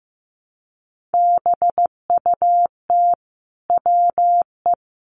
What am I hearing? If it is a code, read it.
Morse code, heard as a keyed tone, about 15 wpm, 700 Hz. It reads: BUT WE
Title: A Sound Of Thunder 15wpm